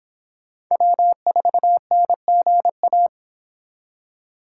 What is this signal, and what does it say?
Morse code, heard as a keyed tone, about 25 wpm, 700 Hz. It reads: W4NGA